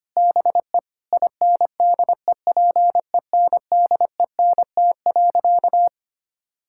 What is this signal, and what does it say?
Morse code, heard as a keyed tone, about 25 wpm, 700 Hz. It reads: BE INDEPENDENT.
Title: Sentences from Top 1000 Words - Review Part 6 - 25wpm